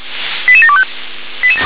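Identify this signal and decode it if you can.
Five-tone selective call (ZVEI), heard as quick runs of five tones, tones of about 70 ms. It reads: 88725
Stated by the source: Five-tone ZVEI selective call, SDR-received, address 88724.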